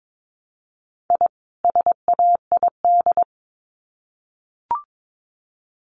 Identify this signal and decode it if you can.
Morse code, heard as a keyed tone, about 22 wpm, 700 Hz. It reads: I SAID
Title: Sets of 2 Words 22wpm